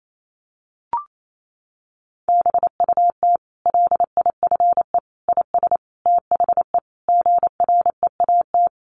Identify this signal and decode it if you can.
Morse code, heard as a keyed tone, about 28 wpm, 690 Hz. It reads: BUT LIFE IS THE GREAT
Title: Sentences from Top 900 Words 28wpm